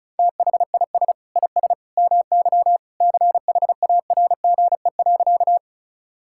Morse code, heard as a keyed tone, about 36 wpm, 700 Hz. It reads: THIS IS MY CHARGE.
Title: Sentences from Top 1000 Words - Review Part 6 - 35wpm